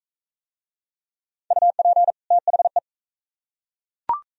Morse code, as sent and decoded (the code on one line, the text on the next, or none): ..- .--. / - .... .
UP THE